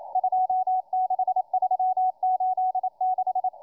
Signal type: Morse code